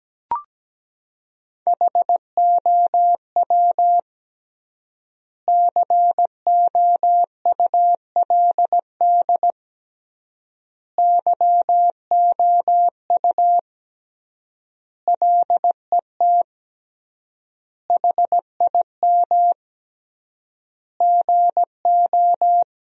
Morse code, heard as a keyed tone, about 17 wpm, 700 Hz. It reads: HOW COULD YOU LET HIM GO
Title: 3x Word-Spacing - Sentences from Top 300 Words - 17wpm